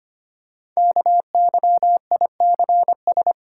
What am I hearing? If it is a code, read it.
Morse code, heard as a keyed tone, about 25 wpm, 700 Hz. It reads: KYICS
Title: Code Groups - 5 Letters - 25wpm